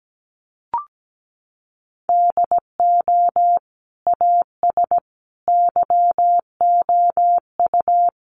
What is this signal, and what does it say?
Morse code, heard as a keyed tone, about 17 wpm, 700 Hz. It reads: DO AS YOU
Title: Sentences from Top 100 Words 17wpm